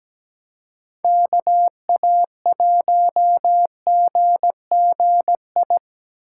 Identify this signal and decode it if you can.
Morse code, heard as a keyed tone, about 20 wpm, 700 Hz. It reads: KA1GGI
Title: Speed Racing - USA 2x3 Call Signs 17wpm